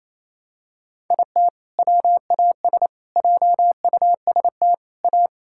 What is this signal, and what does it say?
Morse code, heard as a keyed tone, about 28 wpm, 700 Hz. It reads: IT WAS JUST A
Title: Sentences from Top 1600 Words 28wpm